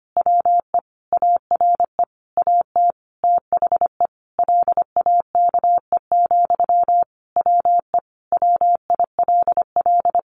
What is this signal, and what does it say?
Morse code, heard as a keyed tone, about 25 wpm, 700 Hz. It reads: WE ARE AT THE LAKE, WE WILL
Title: Sentences from Top 1100 Words 25wpm